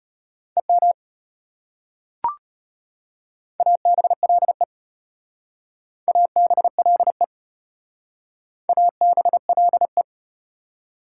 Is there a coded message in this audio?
Yes: Morse code from 0.6 to 10.0 s, heard as a keyed tone, about 34 wpm, 700 Hz. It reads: EM ABLE ABLE ABLE